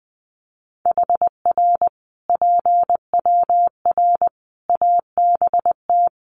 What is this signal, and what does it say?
Morse code, heard as a keyed tone, about 20 wpm, 700 Hz. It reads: HR PWR ABT
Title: Common QSO Phrases 20wpm